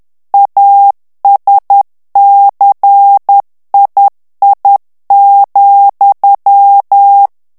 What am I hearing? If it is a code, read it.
Morse code, heard as a keyed tone, about 10 wpm, 800 Hz. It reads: ASCII,
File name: ANGLAIS texte clair 3 vitesse 10 mots minute